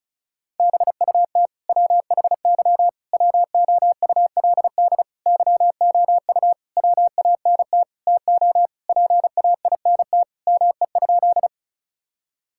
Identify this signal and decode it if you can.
Morse code, heard as a keyed tone, about 34 wpm, 700 Hz. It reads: BUT WHY WOULD YOU WANT TO PAINT ME?
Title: Sentences from Top 1000 Words - Review Part 6 - 35wpm